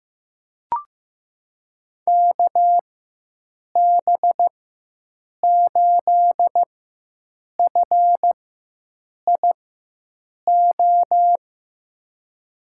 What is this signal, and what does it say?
Morse code, heard as a keyed tone, about 15 wpm, spaced as for 7 wpm, 700 Hz. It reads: KB8FIO
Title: USA Call Signs - 6 Characters - 4x Character Spacing - 15wpm